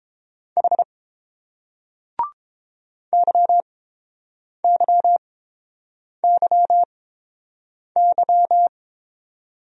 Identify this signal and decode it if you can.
Morse code, heard as a keyed tone, about 30 wpm, spaced as for 7 wpm, 700 Hz. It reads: H YYYY